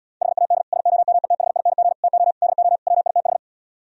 Morse code, heard as a keyed tone, about 77 wpm, 700 Hz. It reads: HAD SOMETHING TO DO WITH